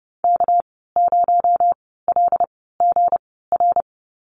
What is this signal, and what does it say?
Morse code, heard as a keyed tone, about 30 wpm, spaced as for 17 wpm, 700 Hz. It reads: K0LGR
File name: Lesson 37g - Letter x - Call Signs 30wpm